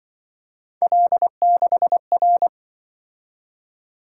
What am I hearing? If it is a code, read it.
Morse code, heard as a keyed tone, about 24 wpm, 700 Hz. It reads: L6R
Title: Speed-Racing - Three Letter-Number Combinations - 20wpm